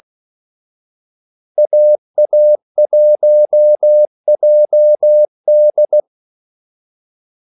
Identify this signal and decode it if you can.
Morse code, heard as a keyed tone, about 16 wpm, 600 Hz. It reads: AA1JD
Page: Kaldesignaler 16 wpm